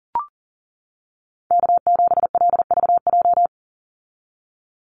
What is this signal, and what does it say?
Morse code, heard as a keyed tone, about 41 wpm, 700 Hz. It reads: K7LVJ
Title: Call Signs - 40wpm